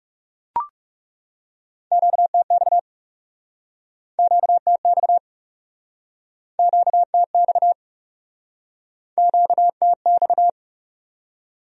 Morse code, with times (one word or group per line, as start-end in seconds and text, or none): 1.9-2.8 QTX
4.2-5.2 QTX
6.6-7.7 QTX
9.2-10.5 QTX